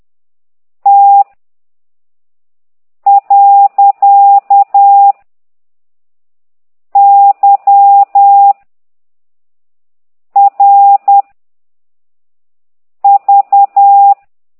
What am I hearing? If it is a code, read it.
Morse code, heard as a keyed tone, about 10 wpm, 800 Hz. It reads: T . Y R V
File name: exerc7